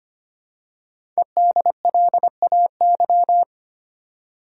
Morse code, heard as a keyed tone, about 25 wpm, 690 Hz. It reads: EDLAY
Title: Code Groups - 5 Letters - 25wpm